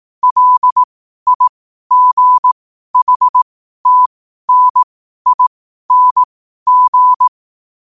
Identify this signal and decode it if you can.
Morse code, heard as a keyed tone, about 18 wpm, spaced as for 12 wpm, 1000 Hz. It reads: LIGHTNING